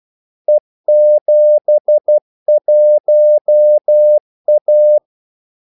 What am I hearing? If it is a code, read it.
Morse code, heard as a keyed tone, about 12 wpm, 600 Hz. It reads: E71A
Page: 2 ekstra ordmellemrum mellem kaldesignalerne